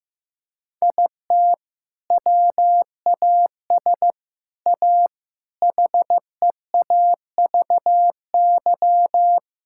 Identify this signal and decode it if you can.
Morse code, heard as a keyed tone, about 15 wpm, 700 Hz. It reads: IT WAS A HEAVY
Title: Sentences from Top 1600 Words 15wpm